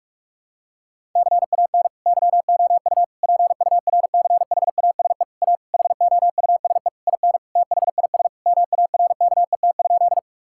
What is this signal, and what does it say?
Morse code, heard as a keyed tone, about 46 wpm, 700 Hz. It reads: CAN YOU PURCHASE A HOUSE IN THIS MARKET?
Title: Sentences from Top 1000 Words - Review Part 5 - 45wpm